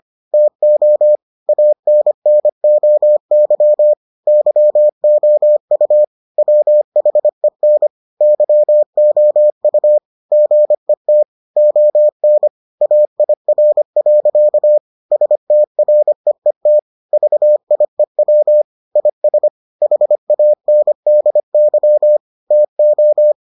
Morse code, heard as a keyed tone, about 25 wpm, 600 Hz. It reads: TO ANNOY YOU WHEN YOU GET ON AIR. STREET VIEW IS HANDY TO